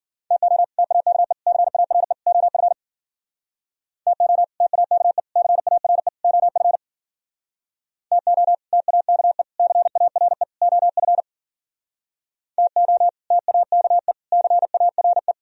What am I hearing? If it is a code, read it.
Morse code, heard as a keyed tone, about 51 wpm, 700 Hz. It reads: TO TAKE CARE OF TO TAKE CARE OF TO TAKE CARE OF TO TAKE CARE